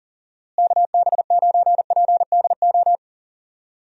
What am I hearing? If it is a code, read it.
Morse code, heard as a keyed tone, about 40 wpm, 700 Hz. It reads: KB9PDO
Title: USA Call Signs - 6 Characters - Repeated 3x - 40wpm